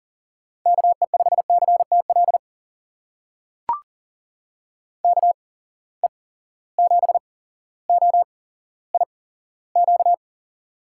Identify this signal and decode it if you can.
Morse code, heard as a keyed tone, about 40 wpm, spaced as for 17 wpm, 700 Hz. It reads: KE5CTL KE7OIQ
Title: USA Call Signs - 6 Characters - 8x Character Spacing - 40wpm